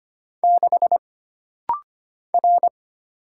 Morse code, heard as a keyed tone, about 25 wpm, 700 Hz. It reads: 6 R
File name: Single Character (Letter:Number:Punctuation:Prosign) - Rapid-Fire - 25wpm